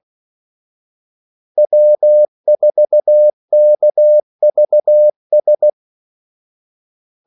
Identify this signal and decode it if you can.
Morse code, heard as a keyed tone, about 16 wpm, 600 Hz. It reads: W4KVS
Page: Kaldesignaler 16 wpm